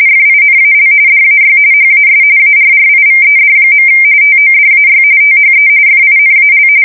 RTTY, about 45.45 baud. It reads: FINAL DAY HAS ARRIVED. TODAY THE FINAL
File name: fsk170 / USS-Missouri